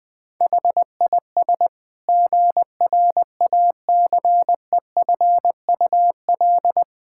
Morse code, heard as a keyed tone, about 21 wpm, 700 Hz. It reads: HIS GRACEFUL
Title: Pangram Challenge v2 - 20wpm